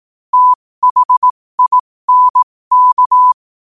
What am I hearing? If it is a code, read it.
Morse code, heard as a keyed tone, about 18 wpm, spaced as for 15 wpm, 1000 Hz. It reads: THINK
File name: qotdfiles / 15wpm